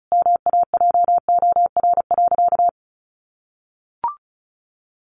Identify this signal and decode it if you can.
Morse code, heard as a keyed tone, about 34 wpm, 700 Hz. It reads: MAJOR.
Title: Sentences from Top 600 Words 35wpm